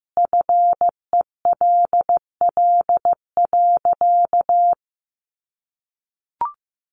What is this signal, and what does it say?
Morse code, heard as a keyed tone, about 14 wpm, 700 Hz. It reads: FELL.
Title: A Sound Of Thunder 15wpm